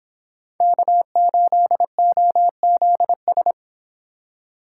Morse code, heard as a keyed tone, about 26 wpm, 700 Hz. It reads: K8OZS